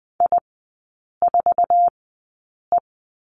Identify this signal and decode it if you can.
Morse code, heard as a keyed tone, about 20 wpm, spaced as for 8 wpm, 700 Hz. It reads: I4E
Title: 060 Lesson 11e - Letter h - Three Characters 20wpm